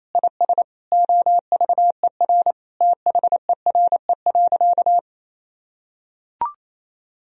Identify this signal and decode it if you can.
Morse code, heard as a keyed tone, about 28 wpm, 700 Hz. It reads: IS OVER THERE.
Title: Sentences from Top 1000 Words - Review Part 1 - 28wpm